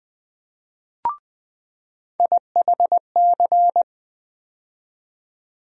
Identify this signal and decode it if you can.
Morse code, heard as a keyed tone, about 20 wpm, 700 Hz. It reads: IHC
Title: USA Call Sign Postfixes - 3 Letters 20wpm